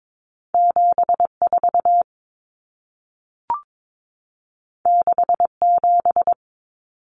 Morse code, heard as a keyed tone, about 22 wpm, 700 Hz. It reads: 74 67
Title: Numbers - 2-Digit 22wpm